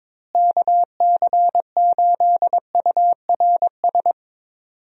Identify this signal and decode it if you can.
Morse code, heard as a keyed tone, about 21 wpm, 700 Hz. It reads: KC8URS
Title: USA Call Signs - 6 Characters - Repeated 3x - 22wpm